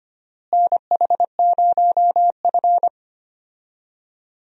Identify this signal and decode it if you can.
Morse code, heard as a keyed tone, about 25 wpm, 700 Hz. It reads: NH0F
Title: USA Call Signs - 4 Characters - Repeated 2x - 25wpm